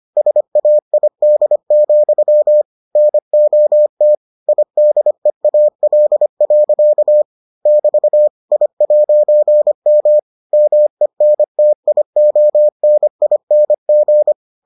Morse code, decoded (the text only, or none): SAID, NOT IDEAL. <BT> I'M MENTIONING